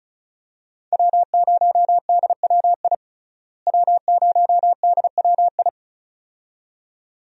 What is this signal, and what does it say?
Morse code, heard as a keyed tone, about 36 wpm, 700 Hz. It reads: W0DWI W0DWI
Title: USA Call Signs - 5 Characters - Repeated 2x - 35wpm